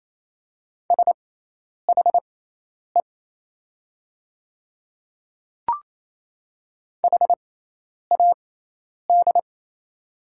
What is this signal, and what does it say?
Morse code, heard as a keyed tone, about 28 wpm, spaced as for 10 wpm, 700 Hz. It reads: SHE HAD
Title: 6x Character Spacing - Top 100 Words - 28wpm